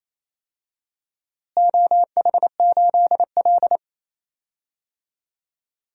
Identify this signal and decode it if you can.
Morse code, heard as a keyed tone, about 28 wpm, 700 Hz. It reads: OH8L
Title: DXCC-100 Call Signs - 4 Characters - Repeated 2x - 28wpm